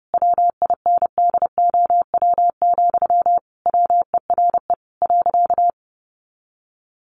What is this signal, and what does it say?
Morse code, heard as a keyed tone, about 30 wpm, 700 Hz. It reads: WINDOW, WERE .
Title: A Sound Of Thunder 30wpm